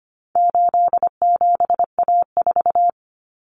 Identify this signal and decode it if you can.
Morse code, heard as a keyed tone, about 25 wpm, 700 Hz. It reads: 87A4
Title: Common Cut Numbers - 4-Digit 25wpm